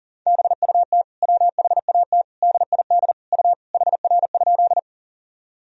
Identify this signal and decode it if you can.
Morse code, heard as a keyed tone, about 37 wpm, 700 Hz. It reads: BUT WHAT DID U HR?
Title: Sentences from Top 300 Words and Abbreviations 40wpm